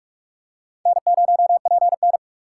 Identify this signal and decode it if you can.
Morse code, heard as a keyed tone, about 46 wpm, 700 Hz. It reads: N0PN
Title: USA Call Signs - 3x Character Spacing - 45wpm